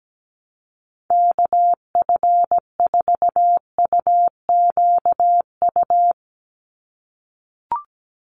Morse code, heard as a keyed tone, about 17 wpm, 700 Hz. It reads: KF4UQU
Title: USA Call Signs 17wpm